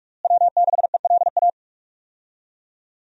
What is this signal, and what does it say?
Morse code, heard as a keyed tone, about 46 wpm, 700 Hz. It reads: W6ELA
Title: USA Call Signs - 5 Characters - Repeated 3x - 45wpm